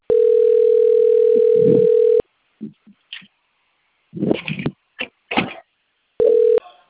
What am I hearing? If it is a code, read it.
Ringback tone (North America), 440 + 480 Hz.